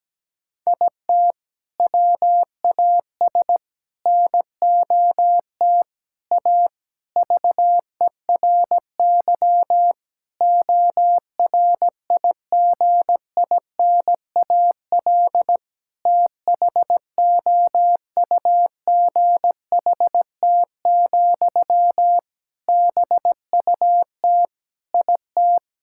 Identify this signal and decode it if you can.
Morse code, heard as a keyed tone, about 17 wpm, 700 Hz. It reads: IT WAS NOT A VERY ORIGINAL THOUGHT, BUT IT
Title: Sentences from Top 1000 Words - Review Part 6 - 17wpm